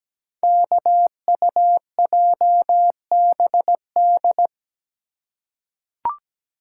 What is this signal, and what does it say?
Morse code, heard as a keyed tone, about 17 wpm, 700 Hz. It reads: KUJBD
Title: Code Groups - 5 Letter:Numbers - 17wpm